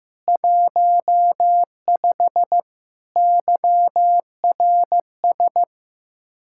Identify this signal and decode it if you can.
Morse code, heard as a keyed tone, about 16 wpm, 700 Hz. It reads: 15 YRS
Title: K1USN SST QSO Practice 15wpm